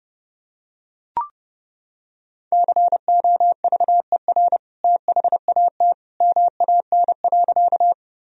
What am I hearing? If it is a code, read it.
Morse code, heard as a keyed tone, about 29 wpm, 700 Hz. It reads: COVER THAT MAN.
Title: Sentences from Top 1000 Words - Review Part 1 - 30wpm